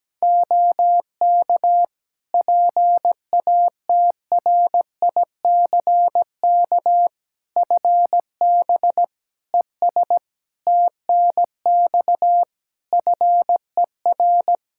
Morse code, heard as a keyed tone, about 17 wpm, 700 Hz. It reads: OK PATRICK FB ES TNX FER
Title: Practice QSO - Exchange 2 - Info - 17wpm